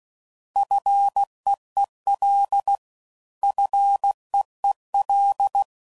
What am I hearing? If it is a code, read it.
Morse code, heard as a keyed tone, about 16 wpm, 800 Hz. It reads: FEEL FEEL
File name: Prefixes / FLs & THs / Single Fs Ls